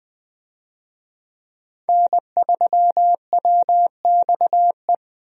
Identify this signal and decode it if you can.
Morse code, heard as a keyed tone, about 20 wpm, 700 Hz. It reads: N3WXE
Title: USA Call Signs - 5 Characters - 20wpm